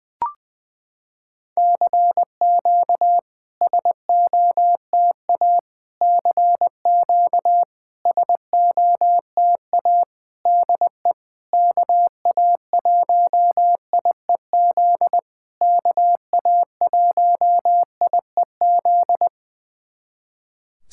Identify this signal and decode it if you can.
Morse code, heard as a keyed tone, about 20 wpm, 700 Hz. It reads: CQ SOTA CQ SOTA DE KA1IEZ KA1IEZ
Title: SOTA QSO Practice 20wpm